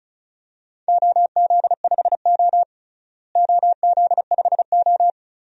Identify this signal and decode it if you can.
Morse code, heard as a keyed tone, about 34 wpm, 700 Hz. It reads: OZ5O OZ5O
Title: DXCC-100 Call Signs - 4 Characters - Repeated 2x - 35wpm